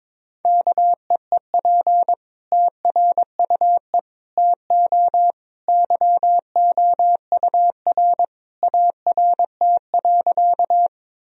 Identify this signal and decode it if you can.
Morse code, heard as a keyed tone, about 21 wpm, 700 Hz. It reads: KEEP TRUE TO YOUR ART.
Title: Sentences from Top 1000 Words - Review Part 3 - 22wpm